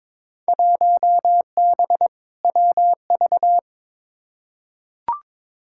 Morse code, heard as a keyed tone, about 22 wpm, 700 Hz. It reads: 1B WV
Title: Field Day Exchanges - 22wpm